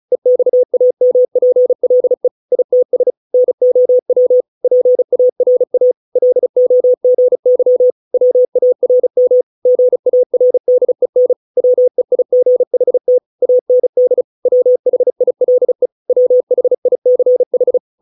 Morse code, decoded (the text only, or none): EXAMPLE ITS NOW PARA LOGY WARM GARDEN WEIGHT AND WHILE WHICH